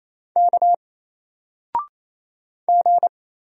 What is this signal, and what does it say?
Morse code, heard as a keyed tone, about 27 wpm, 700 Hz. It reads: K G
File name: Single-Letter-Number-28wpm